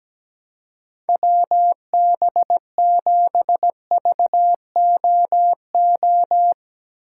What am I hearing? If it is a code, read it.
Morse code, heard as a keyed tone, about 17 wpm, 700 Hz. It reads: WB7VOO
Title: USA Call Signs - 6 Characters - 3x Character Spacing - 17wpm